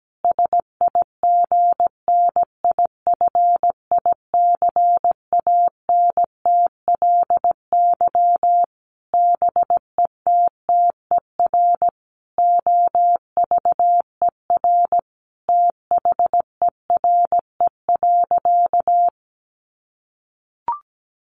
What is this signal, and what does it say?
Morse code, heard as a keyed tone, about 17 wpm, 700 Hz. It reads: SIGNIFICANTLY BETTER OVER THERE.
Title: Sentences from Top 1500 Words 17wpm